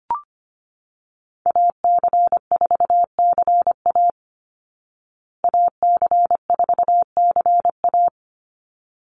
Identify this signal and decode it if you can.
Morse code, heard as a keyed tone, about 25 wpm, 700 Hz. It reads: AC4CA AC4CA
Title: Call Signs Repeated 3x - 25wpm